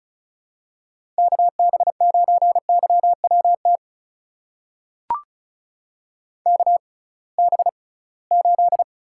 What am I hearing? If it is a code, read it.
Morse code, heard as a keyed tone, about 34 wpm, spaced as for 22 wpm, 700 Hz. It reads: KB9YWT KB8
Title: USA Call Signs - 6 Characters - 6x Character Spacing - 35wpm